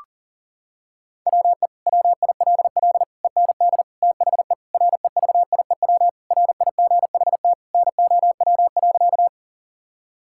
Morse code, heard as a keyed tone, about 40 wpm, 700 Hz. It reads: WE WILL END THE REVIEW RIGHT NOW.
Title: Sentences from Top 1000 Words - Review Part 6 - 40wpm